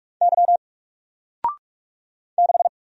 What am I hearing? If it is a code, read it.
Morse code, heard as a keyed tone, about 46 wpm, 700 Hz. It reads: Y 6
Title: Single Letter-Number Alt 45wpm